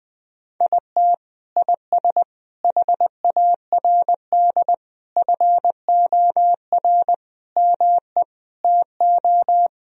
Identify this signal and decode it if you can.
Morse code, heard as a keyed tone, about 20 wpm, 700 Hz. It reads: IT IS HARD FOR ME TO
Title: Sentences from Top 1000 Words - Review Part 3 - 20wpm